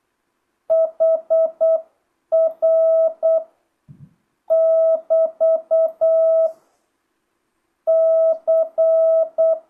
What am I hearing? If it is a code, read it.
Morse code, heard as a keyed tone, about 8 wpm, 650 Hz. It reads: HR <BT> C
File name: jh8cbh / 2025cw